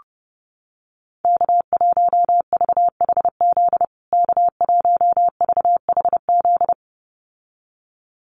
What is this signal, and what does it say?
Morse code, heard as a keyed tone, about 30 wpm, 700 Hz. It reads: K1VHZ K1VHZ
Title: USA Call Signs Repeated 2x 30wpm